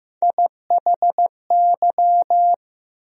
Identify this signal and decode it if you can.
Morse code, heard as a keyed tone, about 15 wpm, 700 Hz. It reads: IHY
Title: USA Call Sign Postfixes - 3 Letters 15wpm